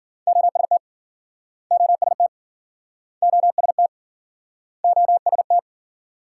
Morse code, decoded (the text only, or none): OST OST OST OST